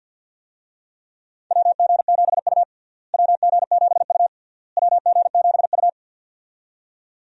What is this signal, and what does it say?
Morse code, heard as a keyed tone, about 51 wpm, 700 Hz. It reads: WG7U WG7U WG7U
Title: USA Call Signs - 4 Characters - Repeated 3x - 50wpm